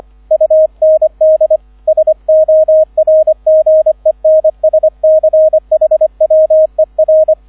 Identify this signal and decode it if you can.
Morse code, heard as a keyed tone, about 25 wpm, 630 Hz, with radio noise. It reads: UND SORGENSCHWER
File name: 001 Erster Streich 25wpm